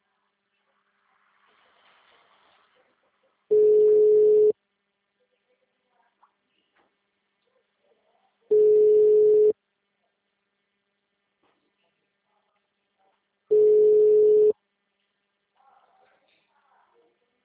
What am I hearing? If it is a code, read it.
Ringback tone (Europe, 425 Hz).